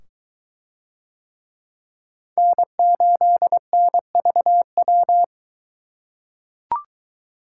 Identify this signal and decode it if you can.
Morse code, heard as a keyed tone, about 21 wpm, 700 Hz. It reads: N8NVW